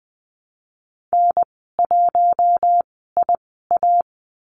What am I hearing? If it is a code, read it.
Morse code, heard as a keyed tone, about 20 wpm, spaced as for 14 wpm, 700 Hz. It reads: N1IA
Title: 048 Lesson 9g - Number 4 - Call Signs 20wpm